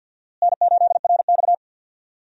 Morse code, heard as a keyed tone, about 53 wpm, 700 Hz. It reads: N8RX
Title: USA Call Signs - 4 Characters - Repeated 3x - 50wpm